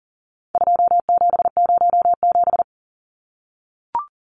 Morse code, heard as a keyed tone, about 40 wpm, 700 Hz. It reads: 2707
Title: Numbers - 4-Digit 40wpm